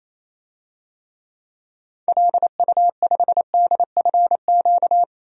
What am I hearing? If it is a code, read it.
Morse code, heard as a keyed tone, about 28 wpm, 700 Hz. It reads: LU5DFQ
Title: DXCC-100 Call Signs - 6 Characters - 28wpm